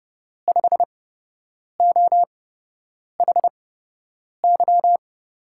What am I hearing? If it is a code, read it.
Morse code, heard as a keyed tone, about 31 wpm, spaced as for 8 wpm, 700 Hz. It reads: 5OHY
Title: USA Call Signs - 6 Characters - 8x Character Spacing - 30wpm